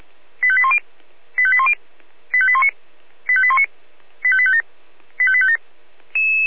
Five-tone selective call (ZVEI), heard as quick runs of five tones, tones of about 70 ms. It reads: 85810 85810 85819 85819 85857 85857